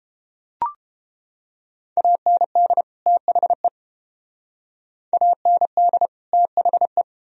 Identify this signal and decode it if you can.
Morse code, heard as a keyed tone, about 29 wpm, 700 Hz. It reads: AND THE AND THE